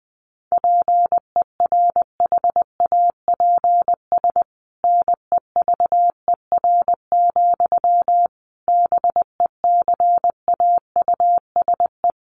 Morse code, heard as a keyed tone, about 20 wpm, 700 Hz. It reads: PERHAPS NEVER, BECAUSE
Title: Sentences from Top 1200 Words 20wpm